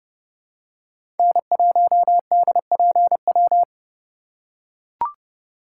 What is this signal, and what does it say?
Morse code, heard as a keyed tone, about 31 wpm, 700 Hz. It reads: N1DPW
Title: USA Call Signs - 5 Characters - 4x Character Spacing - 30wpm